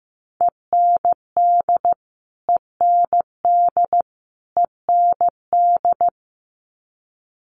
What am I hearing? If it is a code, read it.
Morse code, heard as a keyed tone, about 15 wpm, 700 Hz. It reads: END END END
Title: Top 500 Words Repeated 15wpm